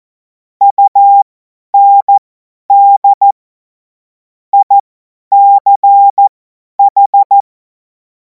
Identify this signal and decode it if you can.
Morse code, heard as a keyed tone, about 14 wpm, spaced as for 10 wpm, 800 Hz. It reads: UND ICH
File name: Morsekurs / Witze / Witz 12. Mai 2026 50 F70